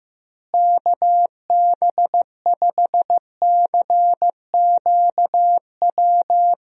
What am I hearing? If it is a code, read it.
Morse code, heard as a keyed tone, about 15 wpm, 700 Hz. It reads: KB5CQW
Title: USA Call Signs - 6 Characters - Repeated 2x - 15wpm